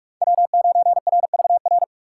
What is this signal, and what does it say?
Morse code, heard as a keyed tone, about 46 wpm, 700 Hz. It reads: W9RVR
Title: USA Call Signs - 5 Characters - 3x Character Spacing - 45wpm